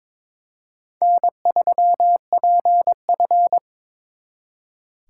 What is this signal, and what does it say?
Morse code, heard as a keyed tone, about 26 wpm, 700 Hz. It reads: N3PF
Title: Speed Racing - USA 1x2 Call Signs 22wpm